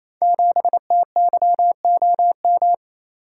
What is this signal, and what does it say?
Morse code, heard as a keyed tone, about 27 wpm, 700 Hz. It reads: 7TYOM
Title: Code Groups - 5 Letter:Numbers - 28wpm